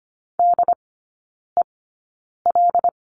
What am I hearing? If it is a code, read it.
Morse code, heard as a keyed tone, about 25 wpm, spaced as for 9 wpm, 690 Hz. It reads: DEL
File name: Lesson 14e - Number 2 - Three Characters 25wpm